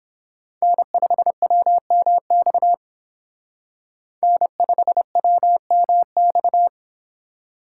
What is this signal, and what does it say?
Morse code, heard as a keyed tone, about 30 wpm, 700 Hz. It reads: N5WMX N5WMX